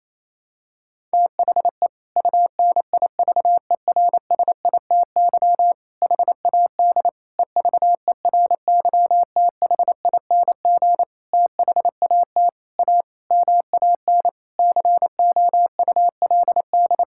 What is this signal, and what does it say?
Morse code, heard as a keyed tone, about 28 wpm, 700 Hz. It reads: THE UNIVERSITY HAD EVERYTHING THAT A MAN COULD
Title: Sentences from Top 1000 Words - Review Part 5 - 28wpm